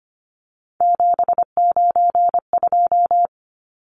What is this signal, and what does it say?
Morse code, heard as a keyed tone, about 25 wpm, 700 Hz. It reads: 792
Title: Numbers - 3-Digit 25wpm